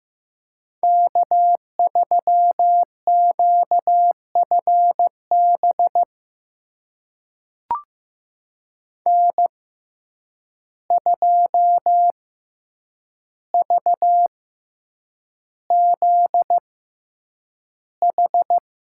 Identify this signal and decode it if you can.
Morse code, heard as a keyed tone, about 15 wpm, spaced as for 8 wpm, 700 Hz. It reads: K3QFB N2VZH
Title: USA Call Signs - 5 Characters - 6x Character Spacing - 15wpm